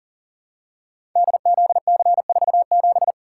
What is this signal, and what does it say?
Morse code, heard as a keyed tone, about 42 wpm, 690 Hz. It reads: DZC47
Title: Code Groups - 5 Letter:Numbers - 40wpm